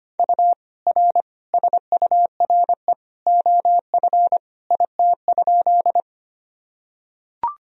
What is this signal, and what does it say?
Morse code, heard as a keyed tone, about 25 wpm, 700 Hz. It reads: U R SURE OF IT?
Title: Sentences from Top 300 Words and Abbreviations 25wpm